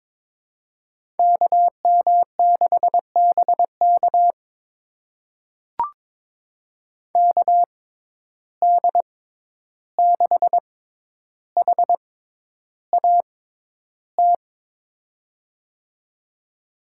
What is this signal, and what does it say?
Morse code, heard as a keyed tone, about 22 wpm, spaced as for 11 wpm, 700 Hz. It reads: KM6BK KD6HAT
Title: USA Call Signs - 6x Character Spacing - 22wpm